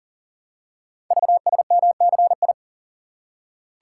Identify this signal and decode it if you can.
Morse code, heard as a keyed tone, about 40 wpm, 700 Hz. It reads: VSMCI
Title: Code Groups - 5 Letter:Numbers - 40wpm